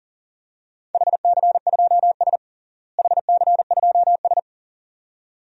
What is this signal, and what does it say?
Morse code, heard as a keyed tone, about 40 wpm, 700 Hz. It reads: HC2S HC2S
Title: DXCC-100 Call Signs - 4 Characters - Repeated 2x - 40wpm